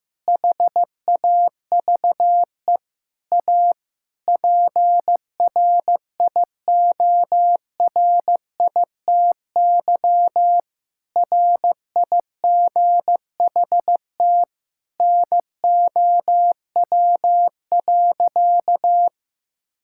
Morse code, heard as a keyed tone, about 15 wpm, 700 Hz. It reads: HAVE A PRIORITY RIGHT NOW.
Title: Sentences from Top 1600 Words 15wpm